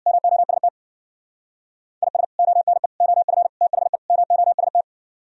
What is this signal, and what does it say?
Morse code, heard as a keyed tone, about 64 wpm, 700 Hz. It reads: MOST IS ONE OF THE MOST